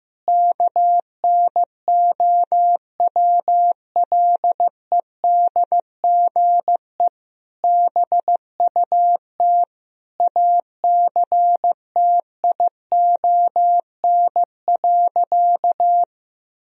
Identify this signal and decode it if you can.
Morse code, heard as a keyed tone, about 16 wpm, 700 Hz. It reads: KNOWLEDGE BUT ACTION.
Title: Sentences from Top 1000 Words - Review Part 4 - 15wpm